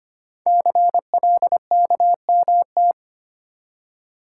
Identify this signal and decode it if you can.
Morse code, heard as a keyed tone, about 25 wpm, 700 Hz. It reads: CLKMT
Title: Code Groups - 5 Letters - 25wpm